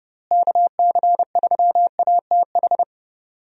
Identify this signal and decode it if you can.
Morse code, heard as a keyed tone, about 30 wpm, 700 Hz. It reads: KC3ATH
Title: USA Call Signs - 6 Characters - Repeated 2x - 30wpm